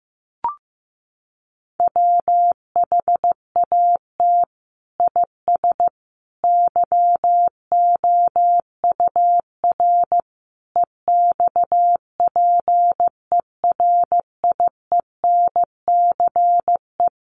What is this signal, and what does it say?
Morse code, heard as a keyed tone, about 15 wpm, 700 Hz. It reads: WHAT IS YOUR EXPERIENCE
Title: Sentences from Top 600 Words 15wpm